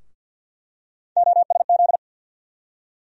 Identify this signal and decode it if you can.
Morse code, heard as a keyed tone, about 46 wpm, 700 Hz. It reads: OSZ
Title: USA Call Sign Postfixes - 3 Letters 50wpm